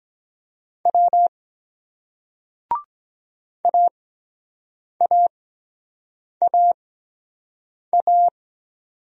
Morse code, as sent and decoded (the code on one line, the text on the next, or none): .-- / .- .- .- .-
W AAAA